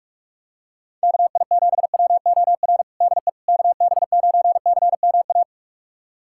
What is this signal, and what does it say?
Morse code, heard as a keyed tone, about 47 wpm, 700 Hz. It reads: KI7WOR DE KB9CMA
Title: USA Callsign DE Callsign - 6 Characters - 45wpm